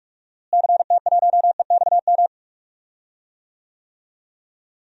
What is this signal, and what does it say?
Morse code, heard as a keyed tone, about 45 wpm, 700 Hz. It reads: CT1EXM
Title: DXCC-100 Call Signs - 6 Characters - Repeated 2x - 45wpm